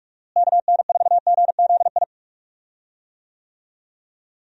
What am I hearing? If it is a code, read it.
Morse code, heard as a keyed tone, about 41 wpm, 700 Hz. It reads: KN4GZI
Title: USA Call Signs - 6 Characters - Repeated 3x - 45wpm